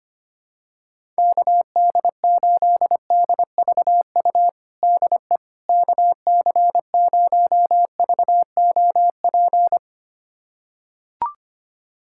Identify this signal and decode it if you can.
Morse code, heard as a keyed tone, about 25 wpm, 700 Hz. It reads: KD8DVU DE KC0VOP
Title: USA Callsign DE Callsign 25wpm